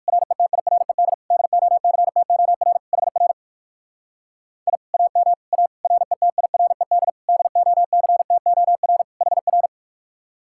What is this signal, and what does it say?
Morse code, heard as a keyed tone, about 59 wpm, 700 Hz. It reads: RETIRED DOCTOR HR I AM A RETIRED DOCTOR HR